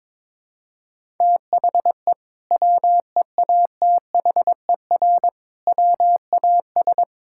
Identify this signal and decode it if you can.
Morse code, heard as a keyed tone, about 22 wpm, 700 Hz. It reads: THE WEATHER WAS
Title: Sentences from Top 1000 Words - Review Part 5 - 22wpm